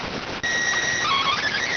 SSTV (slow-scan television), PD 290.